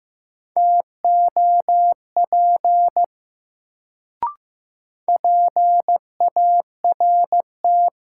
Morse code, heard as a keyed tone, about 15 wpm, 700 Hz. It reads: TOP PART
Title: Sets of 2 Words - New Word Drill - 15wpm